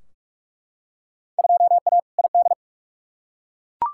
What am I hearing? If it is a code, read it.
Morse code, heard as a keyed tone, about 43 wpm, 700 Hz. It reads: 2A ID
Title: Field Day Exchanges - 45wpm